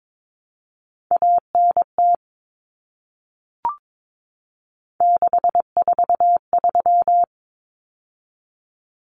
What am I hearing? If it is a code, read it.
Morse code, heard as a keyed tone, about 22 wpm, 700 Hz. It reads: ANT 643
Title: Common Cut Numbers - 3-Digit 22wpm